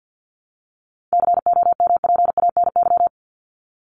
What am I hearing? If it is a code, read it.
Morse code, heard as a keyed tone, about 50 wpm, 700 Hz. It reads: COMPANY
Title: Top 501-750 Words 50wpm